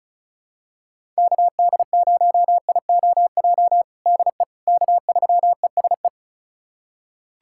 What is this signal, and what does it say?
Morse code, heard as a keyed tone, about 36 wpm, 700 Hz. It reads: KD0IOJ DE K3ESE
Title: USA Callsign DE Callsign 35wpm